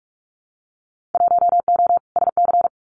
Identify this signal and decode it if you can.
Morse code, heard as a keyed tone, about 46 wpm, 700 Hz. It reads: 1O SC
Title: Winter Field Day Exchanges 45wpm